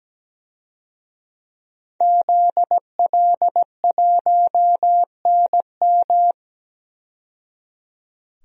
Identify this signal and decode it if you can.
Morse code, heard as a keyed tone, about 17 wpm, 700 Hz. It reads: ZL1NM
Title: DXCC-100 Call Signs - 5 Characters - 17wpm